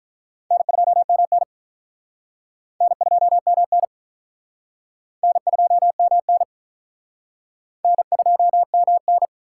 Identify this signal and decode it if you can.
Morse code, heard as a keyed tone, about 44 wpm, 700 Hz. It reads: N2MN N2MN N2MN N2MN